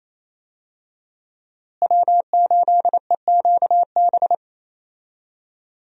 Morse code, heard as a keyed tone, about 28 wpm, 700 Hz. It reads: W8EQB
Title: USA Call Signs - 5 Characters - Repeated 2x - 28wpm